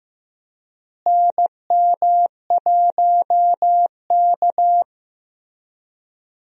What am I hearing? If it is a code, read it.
Morse code, heard as a keyed tone, about 15 wpm, 700 Hz. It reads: NM1K
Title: USA Call Signs - 4 Characters - Repeated 3x - 15wpm